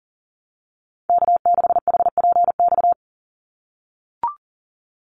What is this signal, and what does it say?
Morse code, heard as a keyed tone, about 40 wpm, 700 Hz. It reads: K6HPX
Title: NA SOTA Chaser Callsigns 40wpm